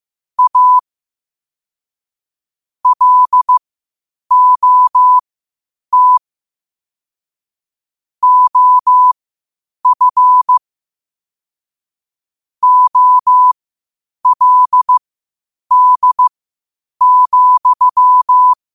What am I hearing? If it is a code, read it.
Morse code, heard as a keyed tone, about 15 wpm, spaced as for 8 wpm, 1000 Hz. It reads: A LOT OF OLD,
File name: qotdfiles / 8wpm